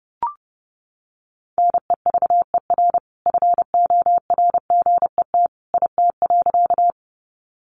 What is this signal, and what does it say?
Morse code, heard as a keyed tone, about 30 wpm, 700 Hz. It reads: NEVER FORGET IT.
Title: Sentences from Top 1200 Words 30wpm